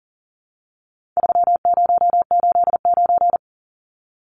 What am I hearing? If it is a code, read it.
Morse code, heard as a keyed tone, about 40 wpm, 700 Hz. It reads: 3089
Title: Numbers - 4-Digit 40wpm